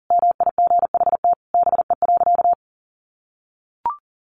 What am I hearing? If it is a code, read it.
Morse code, heard as a keyed tone, about 40 wpm, 700 Hz. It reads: MIGHT BE.
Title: Sentences from Top 1500 Words 40wpm